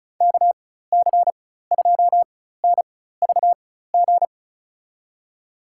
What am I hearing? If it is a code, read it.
Morse code, heard as a keyed tone, about 36 wpm, spaced as for 16 wpm, 700 Hz. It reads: KC2NVG
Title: USA Call Signs - 4x Character Spacing - 35wpm